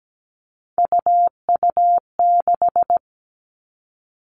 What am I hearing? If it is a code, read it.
Morse code, heard as a keyed tone, about 17 wpm, 700 Hz. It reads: UU6
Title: All Cut Numbers - 3-Digit 17wpm